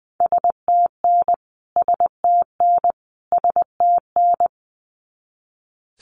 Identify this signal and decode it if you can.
Morse code, heard as a keyed tone, about 20 wpm, 700 Hz. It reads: STN STN STN
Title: Abbreviations Repeated 3x - 20wpm